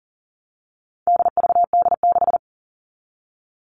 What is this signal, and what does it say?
Morse code, heard as a keyed tone, about 40 wpm, 700 Hz. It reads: DVD6
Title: All Cut Numbers - 4-Digit 40wpm